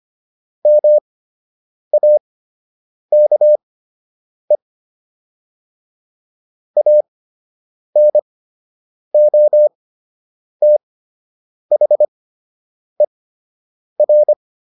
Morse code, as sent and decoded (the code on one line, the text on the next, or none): -- .- -.- . / .- -. --- - .... .-.
MAKE ANOTHER